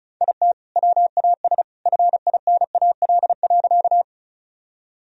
Morse code, heard as a keyed tone, about 34 wpm, 700 Hz. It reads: IT WAS FINAL.
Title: Sentences from Top 1000 Words - Review Part 2 - 35wpm